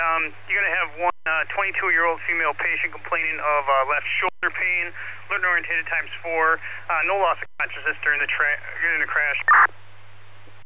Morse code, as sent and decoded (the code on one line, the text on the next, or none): none